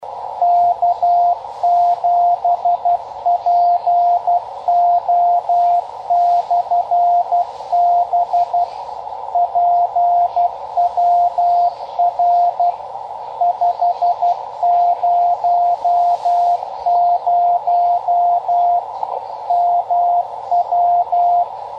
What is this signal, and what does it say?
Morse code, heard as a keyed tone, about 12 wpm, 700 Hz, with radio noise. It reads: K7PO/B PWR 500 MW
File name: Half Watt 10 Meter Beacon